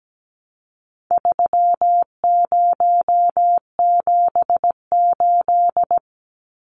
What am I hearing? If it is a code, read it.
Morse code, heard as a keyed tone, about 17 wpm, 700 Hz. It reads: 3078
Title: Numbers - 4-Digit 17wpm